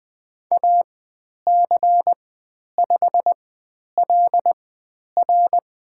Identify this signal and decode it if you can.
Morse code, heard as a keyed tone, about 20 wpm, spaced as for 10 wpm, 700 Hz. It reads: AC5LR
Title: 102 Lesson 17g - Letter u - Call Signs 20wpm